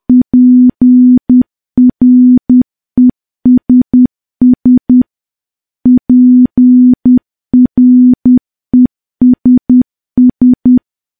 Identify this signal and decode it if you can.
Morse code, heard as a keyed tone, about 10 wpm, 250 Hz. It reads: PRESS PRESS
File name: morse / morse-training-10WPM